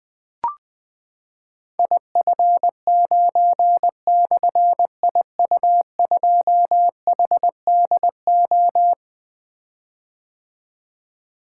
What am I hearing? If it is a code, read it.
Morse code, heard as a keyed tone, about 20 wpm, 700 Hz. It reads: IF9/IU2HDO
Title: Difficult Call Signs 20wpm